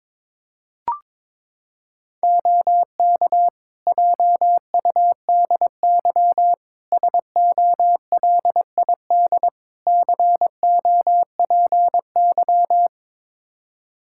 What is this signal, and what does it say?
Morse code, heard as a keyed tone, about 22 wpm, 700 Hz. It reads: OK JUDY SOLID COPY
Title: Practice QSO - Exchange 3 - More Info - 22wpm